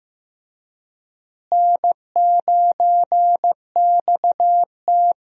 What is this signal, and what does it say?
Morse code, heard as a keyed tone, about 15 wpm, 700 Hz. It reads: N9XT
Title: USA Call Signs - 4 Characters - Repeated 3x - 15wpm